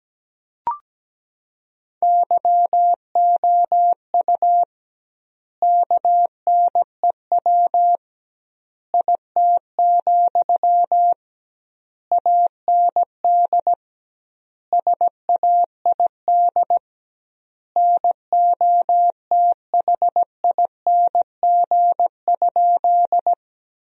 Morse code, heard as a keyed tone, about 17 wpm, 700 Hz. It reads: YOU KNEW IT, AND SAID NOTHING?
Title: 2x Word-Spacing - Sentences from Top 500 Words - 17wpm